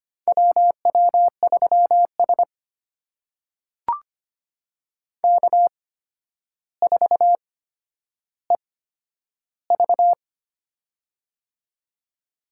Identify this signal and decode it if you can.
Morse code, heard as a keyed tone, about 25 wpm, spaced as for 11 wpm, 700 Hz. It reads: WW3S K4EV
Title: USA Call Signs - 4 Characters - 8x Character Spacing - 25wpm